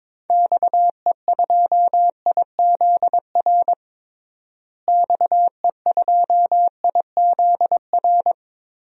Morse code, heard as a keyed tone, about 22 wpm, 700 Hz. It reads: XE2IZR XE2IZR
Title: DXCC-100 Call Signs - 6 Characters - Repeated 2x - 22wpm